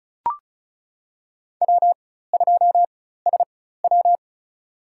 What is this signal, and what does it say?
Morse code, heard as a keyed tone, about 36 wpm, spaced as for 16 wpm, 700 Hz. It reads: W2SW
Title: USA Call Signs - 4 Characters - 4x Character Spacing - 35wpm